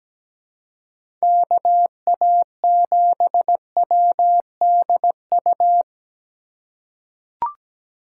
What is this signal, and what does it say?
Morse code, heard as a keyed tone, about 17 wpm, 700 Hz. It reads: KA7WDU
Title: USA Call Signs - 6 Characters - 4x Character Spacing - 17wpm